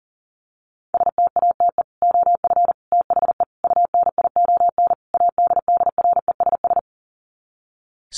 Morse code, heard as a keyed tone, about 40 wpm, 700 Hz. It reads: STATE OF THE UNION ADDRESS
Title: Sets of 5 Words Encore 40wpm